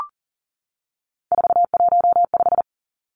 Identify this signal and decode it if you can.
Morse code, heard as a keyed tone, about 40 wpm, 700 Hz. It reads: 415
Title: Numbers - 3-Digit 40wpm